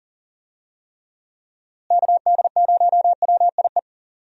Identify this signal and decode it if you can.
Morse code, heard as a keyed tone, about 39 wpm, 700 Hz. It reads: KD0WIE
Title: USA Call Signs - 6 Characters - Repeated 2x - 40wpm